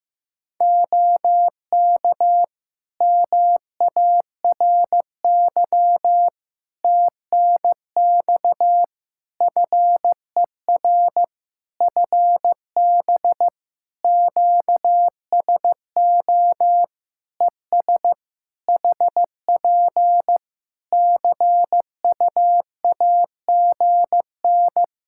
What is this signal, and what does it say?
Morse code, heard as a keyed tone, about 15 wpm, 700 Hz. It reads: OK MARY TNX FER FB QSO ES HP CUAGN
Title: Practice QSO - Exchange 4 - Ending - 15wpm